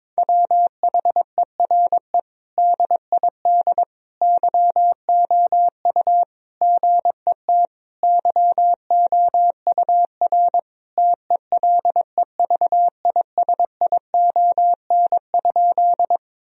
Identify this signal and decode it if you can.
Morse code, heard as a keyed tone, about 22 wpm, 700 Hz. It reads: WHERE DID YOU GET YOUR TELEVISION?
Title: Sentences from Top 1000 Words - Review Part 4 - 22wpm